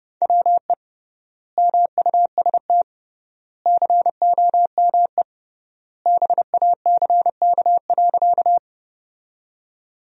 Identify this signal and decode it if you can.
Morse code, heard as a keyed tone, about 31 wpm, 700 Hz. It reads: WE MUST COME BACK.
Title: 3x Word-Spacing - Sentences from Top 200 Words - 30wpm